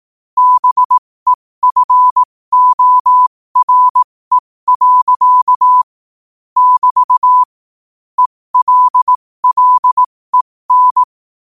Morse code, heard as a keyed tone, about 18 wpm, spaced as for 15 wpm, 1000 Hz. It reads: BEFORE. <BT> ELLEN
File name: qotdfiles / 15wpm